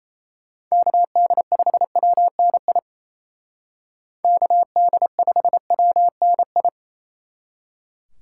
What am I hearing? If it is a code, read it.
Morse code, heard as a keyed tone, about 35 wpm, 700 Hz. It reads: KD5WNI KD5WNI
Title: Speed Racing - USA 2x3 Call Signs 28wpm